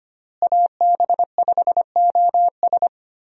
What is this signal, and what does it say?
Morse code, heard as a keyed tone, about 25 wpm, 700 Hz. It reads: AB5OS
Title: USA Call Signs - 5 Characters - Repeated 2x - 25wpm